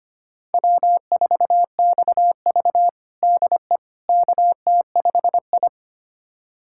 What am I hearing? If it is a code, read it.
Morse code, heard as a keyed tone, about 25 wpm, 700 Hz. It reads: W4XV DE KT5I
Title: USA Callsign DE Callsign - 4 Characters - 25wpm